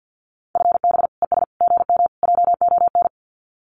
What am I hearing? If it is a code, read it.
Morse code, heard as a keyed tone, about 50 wpm, 700 Hz. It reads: FB ES GM JON
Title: Common QSO Phrases Encore 50wpm